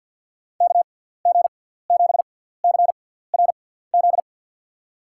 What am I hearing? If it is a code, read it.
Morse code, heard as a keyed tone, about 52 wpm, spaced as for 17 wpm, 700 Hz. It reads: KG7CRZ
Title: USA Call Signs - 6x Character Spacing - 50wpm